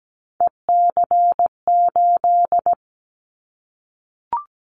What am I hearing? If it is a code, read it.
Morse code, heard as a keyed tone, about 17 wpm, 700 Hz. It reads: EC8
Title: Callsign Prefixes for Top 50 DXCC - 17wpm